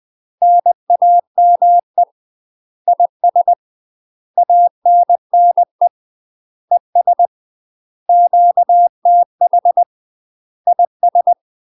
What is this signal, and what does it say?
Morse code, heard as a keyed tone, about 20 wpm, 700 Hz. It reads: NAME IS ANNE ES QTH IS
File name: UK MASTER QSOs TRAINING / 20WPM / UK MASTER QSOs 0040